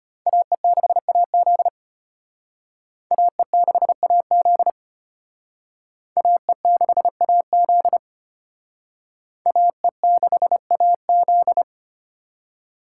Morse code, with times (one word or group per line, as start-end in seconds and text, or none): 0.3-1.7 AE6AZ
3.1-4.7 AE6AZ
6.2-8.0 AE6AZ
9.5-11.6 AE6AZ